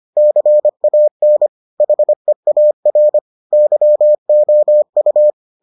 Morse code, heard as a keyed tone, about 25 wpm, 600 Hz. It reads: CAN HEAR YOU